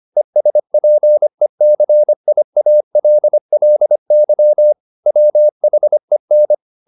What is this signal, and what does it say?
Morse code, heard as a keyed tone, about 25 wpm, 600 Hz. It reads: ESPECIALLY WHEN
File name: the old man and the sea part 1 25wpm